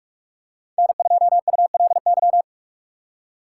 Morse code, heard as a keyed tone, about 46 wpm, 700 Hz. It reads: N2ULY
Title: USA Call Signs - 6x Character Spacing - 45wpm